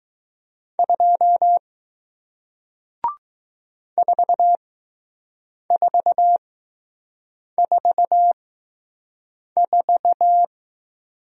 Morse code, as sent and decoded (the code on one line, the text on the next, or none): ..--- / ....- ....- ....- ....-
2 4444